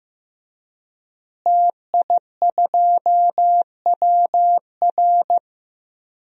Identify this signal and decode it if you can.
Morse code, heard as a keyed tone, about 15 wpm, 700 Hz. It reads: TI2WR
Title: DXCC-100 Call Signs - 5 Characters - Repeated 3x - 15wpm